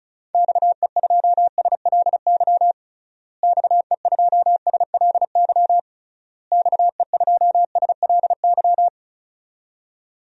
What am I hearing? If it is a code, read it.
Morse code, heard as a keyed tone, about 36 wpm, 700 Hz. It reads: XE2SLY XE2SLY XE2SLY
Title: DXCC-100 Call Signs - 6 Characters - Repeated 3x - 35wpm